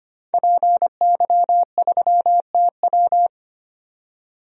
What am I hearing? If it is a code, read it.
Morse code, heard as a keyed tone, about 25 wpm, 700 Hz. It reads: PY3TW
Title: DXCC-100 Call Signs - 5 Characters - 25wpm